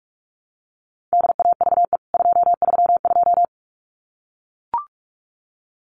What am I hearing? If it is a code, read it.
Morse code, heard as a keyed tone, about 44 wpm, 700 Hz. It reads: DAVE 232
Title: CWT Exchange 45wpm